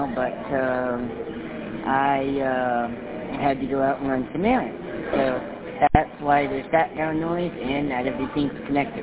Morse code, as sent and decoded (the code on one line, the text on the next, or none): none